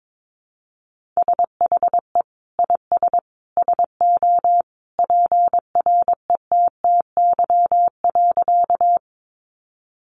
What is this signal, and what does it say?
Morse code, heard as a keyed tone, about 22 wpm, 700 Hz. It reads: SHE IS SO PRETTY.
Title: Sentences from Top 1000 Words 22wpm